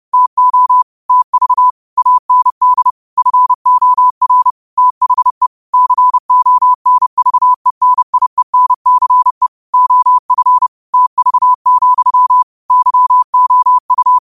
Morse code, heard as a keyed tone, about 30 wpm, 1000 Hz. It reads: TO TV AND FOR THE CONVENIENCE OF TV, YOU
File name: qotdfiles / 30wpm